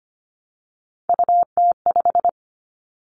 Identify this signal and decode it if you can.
Morse code, heard as a keyed tone, about 25 wpm, 700 Hz. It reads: UT5
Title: Callsign Prefixes for Top 25 DXCC - 25wpm